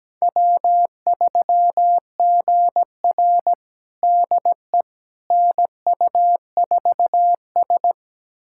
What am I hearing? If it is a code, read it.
Morse code, heard as a keyed tone, about 17 wpm, 700 Hz. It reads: W3GR DE NU4S
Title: USA Callsign DE Callsign - 4 Characters - 17wpm